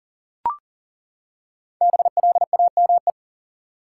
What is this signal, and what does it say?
Morse code, heard as a keyed tone, about 37 wpm, 700 Hz. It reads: BLAME
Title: Five Letter Words - 40wpm